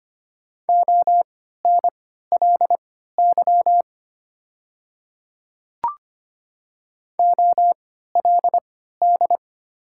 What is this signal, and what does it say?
Morse code, heard as a keyed tone, about 25 wpm, spaced as for 14 wpm, 700 Hz. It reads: ONLY OLD
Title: 3x Character Spacing - Top 200 Words - 25wpm